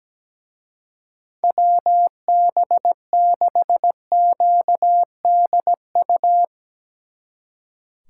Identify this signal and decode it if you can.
Morse code, heard as a keyed tone, about 18 wpm, 700 Hz. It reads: WB6QDU
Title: Speed Racing - USA 2x3 Call Signs 17wpm